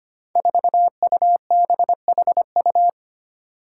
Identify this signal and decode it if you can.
Morse code, heard as a keyed tone, about 25 wpm, 700 Hz. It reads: 4UBHU
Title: Five Challenging Letter-Number Combinations 25wpm